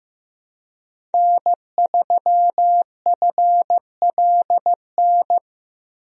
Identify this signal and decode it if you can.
Morse code, heard as a keyed tone, about 15 wpm, 700 Hz. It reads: N3FLN
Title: USA Call Signs - 5 Characters - Repeated 2x - 15wpm